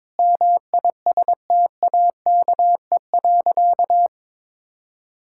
Morse code, heard as a keyed tone, about 22 wpm, 700 Hz. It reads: MISTAKE.
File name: practice / 22wpm / Sentences with 4-Word Phrases - 22wpm